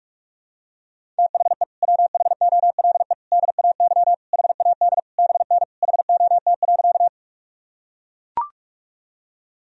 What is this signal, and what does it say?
Morse code, heard as a keyed tone, about 42 wpm, 690 Hz. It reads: THE WHOLE DAY HAD BN HOT.
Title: Sentences from Top 500 Words and Abbreviations 45wpm